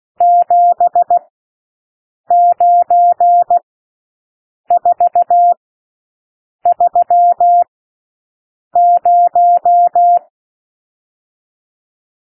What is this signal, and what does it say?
Morse code, heard as a keyed tone, about 16 wpm, spaced as for 6 wpm, 700 Hz. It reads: 79430
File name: SOUND NUMBERS 16 wpm